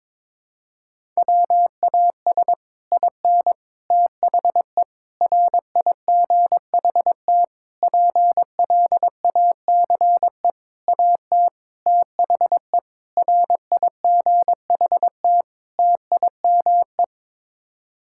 Morse code, heard as a keyed tone, about 22 wpm, 700 Hz. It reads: WAS IN THE RIGHT PLACE AT THE RIGHT TIME
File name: Sets of 9 Words - 22wpm